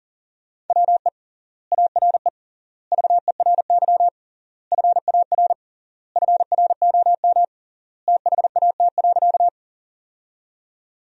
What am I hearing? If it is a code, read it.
Morse code, heard as a keyed tone, about 40 wpm, 700 Hz. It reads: WE ARE VERY FAR FROM THAT.
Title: 3x Word-Spacing - Sentences from Top 300 Words - 40wpm